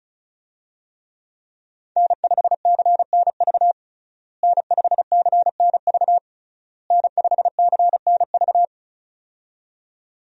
Morse code, heard as a keyed tone, about 34 wpm, 700 Hz. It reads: N5CNV N5CNV N5CNV
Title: USA Call Signs - 5 Characters - Repeated 3x - 35wpm